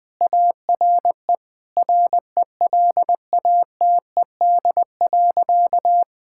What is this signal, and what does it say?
Morse code, heard as a keyed tone, about 19 wpm, 700 Hz. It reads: ARE RELATED.
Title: Sentences from Top 1000 Words - Review Part 5 - 20wpm